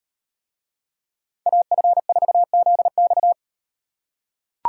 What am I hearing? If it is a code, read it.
Morse code, heard as a keyed tone, about 36 wpm, 700 Hz. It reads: AF4ZX